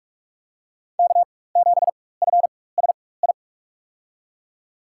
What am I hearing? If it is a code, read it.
Morse code, heard as a keyed tone, about 46 wpm, spaced as for 21 wpm, 700 Hz. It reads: K7FSI
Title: USA Call Signs - 5 Characters - 4x Character Spacing - 45wpm